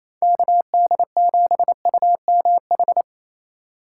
Morse code, heard as a keyed tone, about 28 wpm, 700 Hz. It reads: KD7UMH
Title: USA Call Signs - 6 Characters - 3x Character Spacing - 28wpm